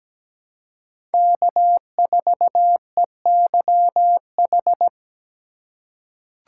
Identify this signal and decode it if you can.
Morse code, heard as a keyed tone, about 17 wpm, 700 Hz. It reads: K4EYH
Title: USA Call Signs - 5 Characters - 4x Character Spacing - 17wpm